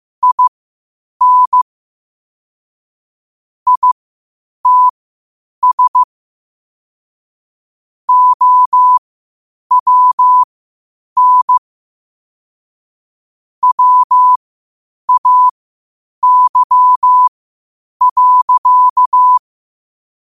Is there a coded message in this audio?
Yes: Morse code from 0.2 to 19.4 s, heard as a keyed tone, about 15 wpm, spaced as for 8 wpm, 1000 Hz. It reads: IN ITS OWN WAY.